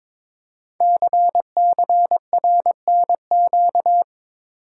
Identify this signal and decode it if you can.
Morse code, heard as a keyed tone, about 22 wpm, 700 Hz. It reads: CCRNQ
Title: Code Groups - 5 Letter:Numbers - 22wpm